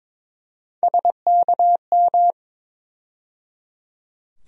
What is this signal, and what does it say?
Morse code, heard as a keyed tone, about 22 wpm, 700 Hz. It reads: SKM
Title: USA Call Sign Postfixes - 3 Letters 22wpm